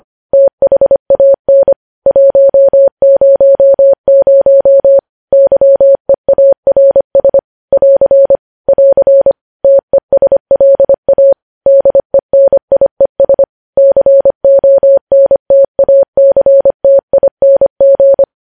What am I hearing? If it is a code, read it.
Morse code, heard as a keyed tone, about 25 wpm, 550 Hz. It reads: THAN 100 YEARS <AR> <AR> TESLA DENIES CONTACTING
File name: podcast / News headlines-1x-25x25